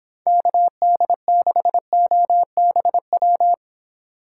Morse code, heard as a keyed tone, about 25 wpm, 700 Hz. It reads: KD6OBW